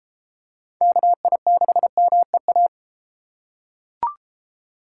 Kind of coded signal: Morse code

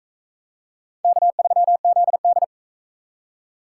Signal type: Morse code

